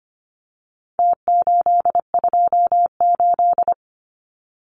Morse code, heard as a keyed tone, about 25 wpm, 700 Hz. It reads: T828
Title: Numbers - Contest Style - 4-Digit 25wpm